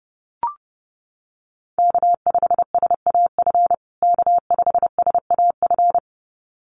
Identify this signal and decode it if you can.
Morse code, heard as a keyed tone, about 30 wpm, 700 Hz. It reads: K5SAF K5SAF
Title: USA Call Signs Repeated 2x 30wpm